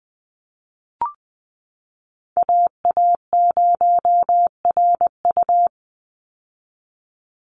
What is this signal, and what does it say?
Morse code, heard as a keyed tone, about 20 wpm, 700 Hz. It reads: AA0RU
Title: USA Call Signs 20wpm